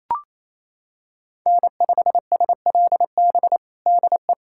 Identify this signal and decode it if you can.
Morse code, heard as a keyed tone, about 27 wpm, 700 Hz. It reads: N5SLB DE
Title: USA Callsign DE Callsign 28wpm